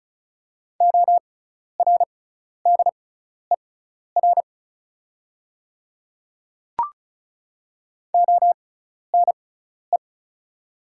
Morse code, heard as a keyed tone, about 34 wpm, spaced as for 12 wpm, 690 Hz. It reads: ORDER ONE
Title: 6x Character Spacing - Top 500 Words - 35wpm